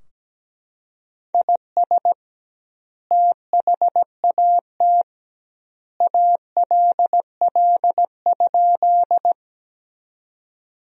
Morse code, heard as a keyed tone, about 17 wpm, 700 Hz. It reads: IS THAT ALL?
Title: 2x Word-Spacing - Sentences from Top 100 Words - 17wpm